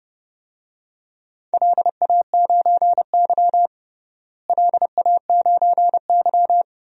Morse code, heard as a keyed tone, about 31 wpm, 700 Hz. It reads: LA9Y LA9Y
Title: DXCC-100 Call Signs - 4 Characters - Repeated 3x - 30wpm